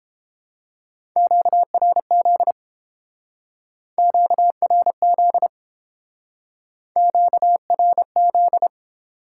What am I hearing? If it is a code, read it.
Morse code, heard as a keyed tone, about 29 wpm, 700 Hz. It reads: QRZ QRZ QRZ